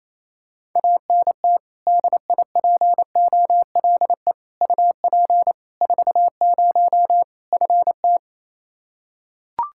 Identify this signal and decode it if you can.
Morse code, heard as a keyed tone, about 28 wpm, 700 Hz. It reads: ANT DIPOLE UP 40 FT
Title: Practice QSO - Exchange 2 - Info - 28wpm